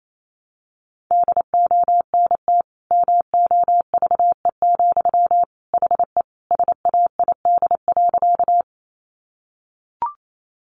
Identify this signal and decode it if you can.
Morse code, heard as a keyed tone, about 28 wpm, 700 Hz. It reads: DONT MOVE, HE SAID.
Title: Sentences from Top 300 Words 28wpm